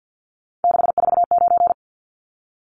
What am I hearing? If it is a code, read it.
Morse code, heard as a keyed tone, about 50 wpm, 700 Hz. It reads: 649
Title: Numbers - 3-Digit 50wpm